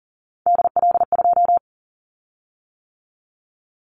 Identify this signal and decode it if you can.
Morse code, heard as a keyed tone, about 40 wpm, 700 Hz. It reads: DL2
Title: Callsign Prefixes for Top 25 DXCC - 40wpm